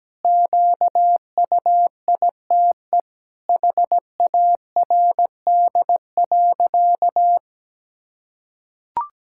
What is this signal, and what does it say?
Morse code, heard as a keyed tone, about 17 wpm, 700 Hz. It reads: QUITE HARD.
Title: Pangram Challenge v2 - 17wpm